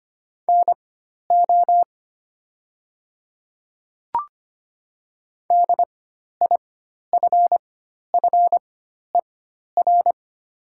Morse code, heard as a keyed tone, about 25 wpm, spaced as for 12 wpm, 700 Hz. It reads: NO DIFFER
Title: 4x Character Spacing - Top 200 Words - 25wpm